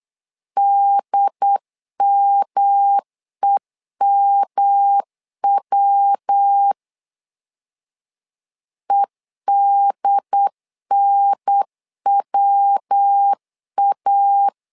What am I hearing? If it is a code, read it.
Morse code, heard as a keyed tone, about 8 wpm, 800 Hz. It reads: DMEMW EDNWA
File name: LEZIONI / 3 MWRO / 3 MW 8wpm 2501